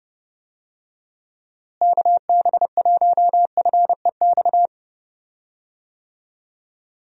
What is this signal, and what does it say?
Morse code, heard as a keyed tone, about 30 wpm, 700 Hz. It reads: KB1FEX
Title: USA Call Signs - 6 Characters - Repeated 2x - 30wpm